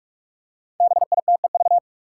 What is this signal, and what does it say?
Morse code, heard as a keyed tone, about 45 wpm, 700 Hz. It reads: BITEV
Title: Five Challenging Letter-Number Combinations 45wpm